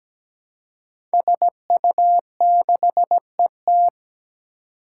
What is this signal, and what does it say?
Morse code, heard as a keyed tone, about 17 wpm, 700 Hz. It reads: SU6ET
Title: Five Challenging Letter-Number Combinations 17wpm